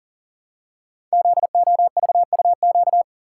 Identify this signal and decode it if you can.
Morse code, heard as a keyed tone, about 40 wpm, 700 Hz. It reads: ZOVUQ
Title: Code Groups - 5 Letters - 40wpm